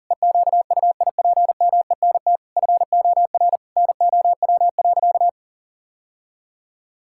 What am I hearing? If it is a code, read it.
Morse code, heard as a keyed tone, about 40 wpm, 700 Hz. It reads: EQUIPMENT FOR NOW.
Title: Sentences from Top 1000 Words - Review Part 5 - 40wpm